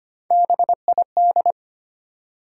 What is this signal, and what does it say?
Morse code, heard as a keyed tone, about 25 wpm, 700 Hz. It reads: BID
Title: Three Letter Words - 25wpm